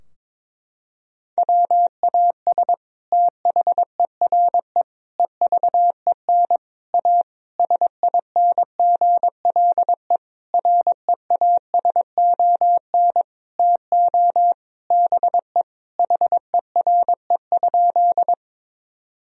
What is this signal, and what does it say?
Morse code, heard as a keyed tone, about 22 wpm, 700 Hz. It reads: WAS THERE EVEN A SINGLE REASON TO BE HERE?
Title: Sentences from Top 1000 Words - Review Part 4 - 22wpm